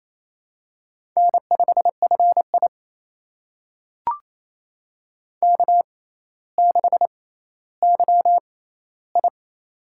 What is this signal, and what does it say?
Morse code, heard as a keyed tone, about 28 wpm, spaced as for 14 wpm, 700 Hz. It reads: N5FI K6YI
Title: USA Call Signs - 4 Characters - 6x Character Spacing - 28wpm